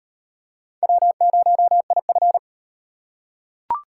Morse code, heard as a keyed tone, about 38 wpm, 700 Hz. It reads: W0IF